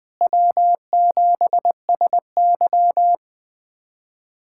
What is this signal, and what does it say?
Morse code, heard as a keyed tone, about 20 wpm, 700 Hz. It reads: W7SY
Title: USA Callsign DE Callsign - 4 Characters - 20wpm